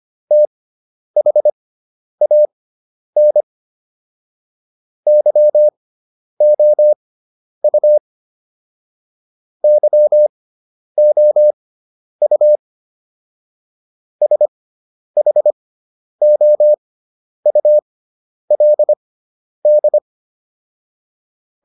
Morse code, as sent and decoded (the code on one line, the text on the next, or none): - .... .- -. / -.-- --- ..- / -.-- --- ..- / ... .... --- ..- .-.. -..
THAN YOU YOU SHOULD